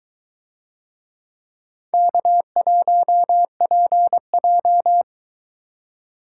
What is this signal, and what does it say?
Morse code, heard as a keyed tone, about 22 wpm, 700 Hz. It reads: K1PJ